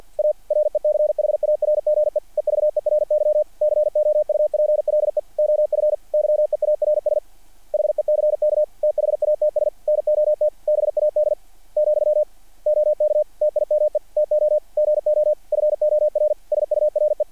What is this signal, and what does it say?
Morse code, heard as a keyed tone, about 48 wpm, 600 Hz. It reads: A RECHARGE EVERY COUPLE OF YEARS HECK THATS NOT BAD , OK TIME TO GO FOR SURE
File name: archived / CRSnet